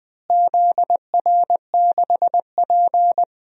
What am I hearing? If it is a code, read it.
Morse code, heard as a keyed tone, about 21 wpm, 700 Hz. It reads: ZR6P
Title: DXCC-100 Call Signs - 4 Characters - Repeated 3x - 20wpm